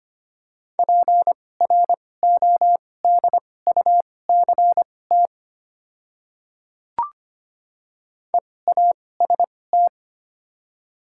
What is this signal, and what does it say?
Morse code, heard as a keyed tone, about 25 wpm, spaced as for 18 wpm, 700 Hz. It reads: PRODUCT EAST
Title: 2x Character Spacing - Top 500 Words - 25wpm